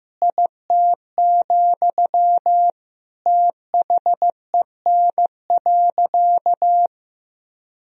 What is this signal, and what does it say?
Morse code, heard as a keyed tone, about 15 wpm, 700 Hz. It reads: IT, THEN.
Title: Sentences from Top 200 Words and Abbreviations 15wpm